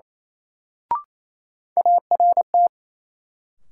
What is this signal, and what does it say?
Morse code, heard as a keyed tone, about 28 wpm, 700 Hz. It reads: ART
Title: Sentences from Top 600 Words - New Word Drill - 28wpm